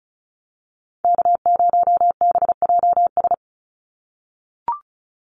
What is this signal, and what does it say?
Morse code, heard as a keyed tone, about 34 wpm, 700 Hz. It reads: K0BJS
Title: USA Call Signs 35wpm